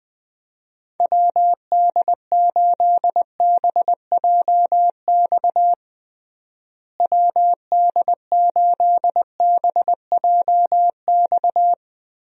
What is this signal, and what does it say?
Morse code, heard as a keyed tone, about 21 wpm, 700 Hz. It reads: WD8BJX WD8BJX
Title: USA Call Signs - 6 Characters - Repeated 3x - 20wpm